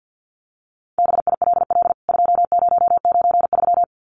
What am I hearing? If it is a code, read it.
Morse code, heard as a keyed tone, about 49 wpm, 700 Hz. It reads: BILL 2093
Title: CWT Exchange 50wpm